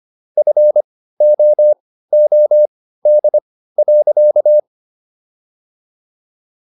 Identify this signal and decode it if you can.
Morse code, heard as a keyed tone, about 25 wpm, spaced as for 15 wpm, 600 Hz. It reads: FOOD.